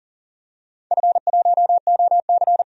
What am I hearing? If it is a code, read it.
Morse code, heard as a keyed tone, about 42 wpm, 690 Hz. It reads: F1OC
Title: DXCC-100 Call Signs - 4 Characters - 40wpm